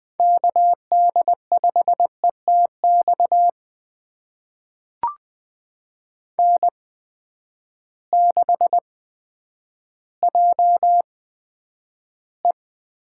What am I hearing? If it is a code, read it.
Morse code, heard as a keyed tone, about 20 wpm, spaced as for 10 wpm, 700 Hz. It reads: KD5ETX N6JE
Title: USA Call Signs - 8x Character Spacing - 20wpm